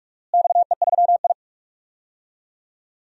Morse code, heard as a keyed tone, about 42 wpm, 690 Hz. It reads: XE3I
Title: DXCC-100 Call Signs - 4 Characters - Repeated 3x - 45wpm